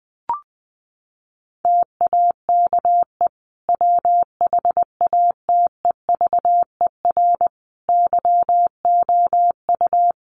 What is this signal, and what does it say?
Morse code, heard as a keyed tone, about 20 wpm, 700 Hz. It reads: TAKE WHATEVER YOU
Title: Sentences from Top 1100 Words 20wpm